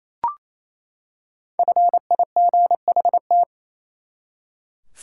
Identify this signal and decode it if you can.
Morse code, heard as a keyed tone, about 28 wpm, 700 Hz. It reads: FIGHT
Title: Five Letter Words - 28wpm